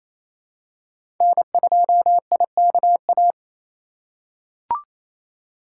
Morse code, heard as a keyed tone, about 28 wpm, 700 Hz. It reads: N2IKA
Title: USA Call Signs - 3x Character Spacing - 28wpm